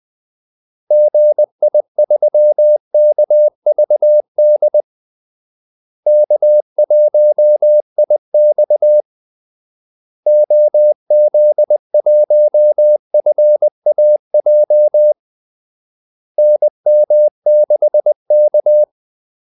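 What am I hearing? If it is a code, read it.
Morse code, heard as a keyed tone, about 20 wpm, 600 Hz. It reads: GI3KVD K1IX OZ1FAJ NM6K
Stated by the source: Morse call-sign practice at 20 wpm.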